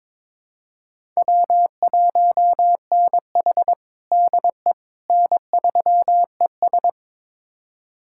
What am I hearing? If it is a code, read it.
Morse code, heard as a keyed tone, about 22 wpm, 700 Hz. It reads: W1NH DE N3ES
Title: USA Callsign DE Callsign - 4 Characters - 22wpm